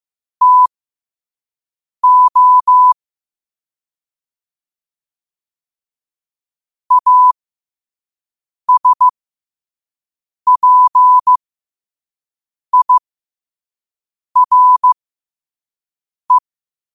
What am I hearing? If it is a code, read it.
Morse code, heard as a keyed tone, about 15 wpm, spaced as for 5 wpm, 1000 Hz. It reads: TO ASPIRE
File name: qotdfiles / 5wpm